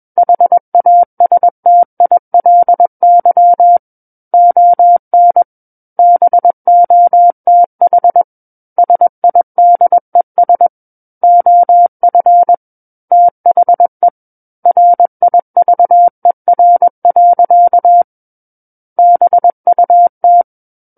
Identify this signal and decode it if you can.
Morse code, heard as a keyed tone, about 21 wpm, 700 Hz. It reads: HASTILY ON BOTH SIDES OF THE RIVER. BUT
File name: Book One / 12-Chapter 12 - 21 WPM